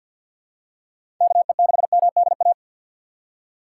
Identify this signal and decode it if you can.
Morse code, heard as a keyed tone, about 47 wpm, 700 Hz. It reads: KE6MDA
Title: USA Call Signs - 4x Character Spacing - 50wpm